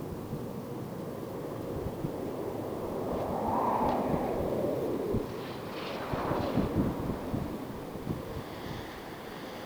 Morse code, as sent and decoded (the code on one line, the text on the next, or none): none